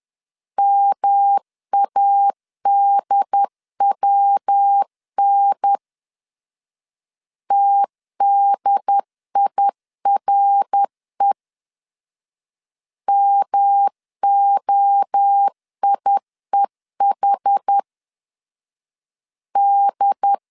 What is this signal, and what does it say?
Morse code, heard as a keyed tone, about 10 wpm, 800 Hz. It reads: MADWN TDIRE MOIEH D
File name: CW / DET / 1 3 1500C S C 10wpm 3335